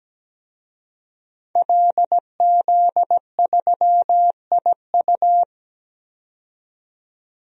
Morse code, heard as a keyed tone, about 17 wpm, 700 Hz. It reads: LZ3IU
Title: DXCC-100 Call Signs - 5 Characters - 17wpm